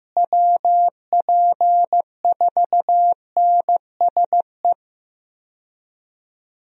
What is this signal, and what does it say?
Morse code, heard as a keyed tone, about 15 wpm, 700 Hz. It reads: WP4NSE
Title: USA Call Signs - 6 Characters - Repeated 2x - 15wpm